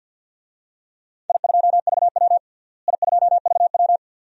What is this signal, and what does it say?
Morse code, heard as a keyed tone, about 53 wpm, 700 Hz. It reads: I2VW I2VW
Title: DXCC-100 Call Signs - 4 Characters - Repeated 2x - 50wpm